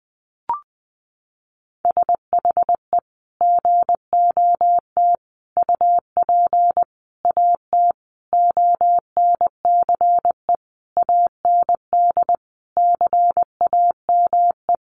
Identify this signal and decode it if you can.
Morse code, heard as a keyed tone, about 20 wpm, 700 Hz. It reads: SHE GOT UP AT ONCE AND CAME
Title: Sentences from Top 1100 Words 20wpm